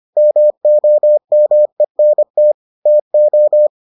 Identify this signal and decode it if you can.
Morse code, heard as a keyed tone, about 25 wpm, 600 Hz. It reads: MOMENT TO